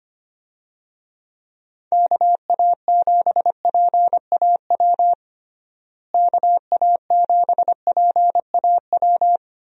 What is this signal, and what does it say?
Morse code, heard as a keyed tone, about 25 wpm, 700 Hz. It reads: KA7PAW KA7PAW
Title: USA Call Signs - 6 Characters - Repeated 3x - 25wpm